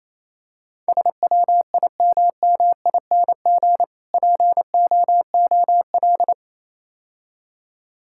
Morse code, heard as a keyed tone, about 27 wpm, 700 Hz. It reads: SWIMMING POOL
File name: Easy Words - Speed-Racing with Hints - 22wpm